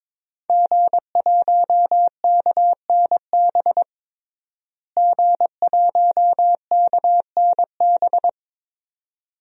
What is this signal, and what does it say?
Morse code, heard as a keyed tone, about 23 wpm, 700 Hz. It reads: G1KNB G1KNB
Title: DXCC-100 Call Signs - 5 Characters - Repeated 2x - 22wpm